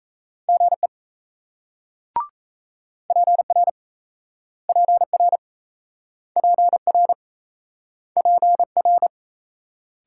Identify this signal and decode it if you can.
Morse code, heard as a keyed tone, about 38 wpm, 700 Hz. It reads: GE PR PR PR PR